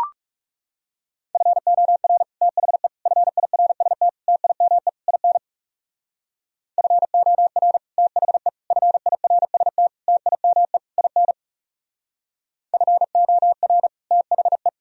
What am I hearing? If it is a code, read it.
Morse code, heard as a keyed tone, about 35 wpm, 700 Hz. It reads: FOR THE FIRST TIME IN FOR THE FIRST TIME IN FOR THE
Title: Speed-Racing - Sets of 5 Words 30wpm